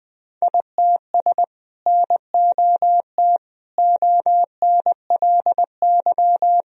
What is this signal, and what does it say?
Morse code, heard as a keyed tone, about 20 wpm, 700 Hz. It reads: ITS NOT ONLY
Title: Sentences from Top 1000 Words - Review Part 2 - 20wpm